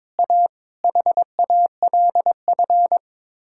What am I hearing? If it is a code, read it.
Morse code, heard as a keyed tone, about 22 wpm, 700 Hz. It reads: A HALF
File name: practice / 22wpm / Build A Phrase - 5-Word Phrases - 22wpm